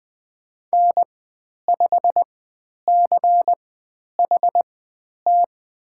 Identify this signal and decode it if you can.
Morse code, heard as a keyed tone, about 20 wpm, spaced as for 10 wpm, 700 Hz. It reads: N5CHT
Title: 102 Lesson 17g - Letter u - Call Signs 20wpm